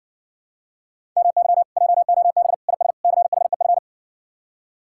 Morse code, heard as a keyed tone, about 60 wpm, 700 Hz. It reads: MY JOB IS CHEF